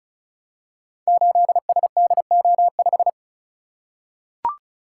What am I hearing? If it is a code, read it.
Morse code, heard as a keyed tone, about 36 wpm, 700 Hz. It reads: 8SDO5
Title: Code Groups - 5 Letter:Numbers - 35wpm